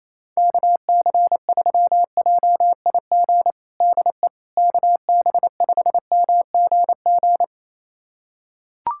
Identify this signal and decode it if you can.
Morse code, heard as a keyed tone, about 28 wpm, 700 Hz. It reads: KC3JIG DE KB5MGG
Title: USA Callsign DE Callsign 28wpm